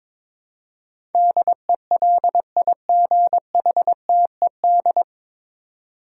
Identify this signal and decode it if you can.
Morse code, heard as a keyed tone, about 22 wpm, 700 Hz. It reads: DELIGHTED
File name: 9 Letter Words - First 3 Letters Given - 22wpm